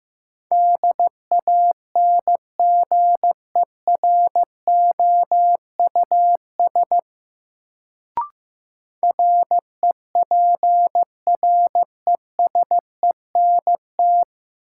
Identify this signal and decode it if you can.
Morse code, heard as a keyed tone, about 15 wpm, 700 Hz. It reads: DANGEROUS REPRESENT
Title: Sentences from Top 1200 Words - New Word Drill - 15wpm